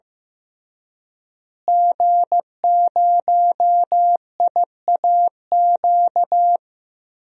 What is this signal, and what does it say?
Morse code, heard as a keyed tone, about 15 wpm, 700 Hz. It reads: G0IAQ
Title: DXCC-100 Call Signs - 5 Characters - Repeated 3x - 15wpm